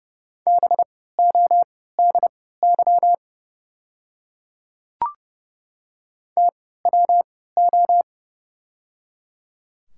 Morse code, heard as a keyed tone, about 30 wpm, spaced as for 17 wpm, 700 Hz. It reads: BODY TWO
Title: 3x Character Spacing - Top 500 Words - 30wpm